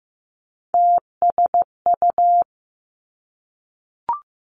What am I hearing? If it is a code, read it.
Morse code, heard as a keyed tone, about 15 wpm, 700 Hz. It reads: TSU
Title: Three Challenging Letter-Number Combinations 15wpm